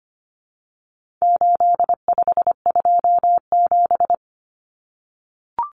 Morse code, heard as a keyed tone, about 25 wpm, 700 Hz. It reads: 8527
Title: Common Cut Numbers - 4-Digit 25wpm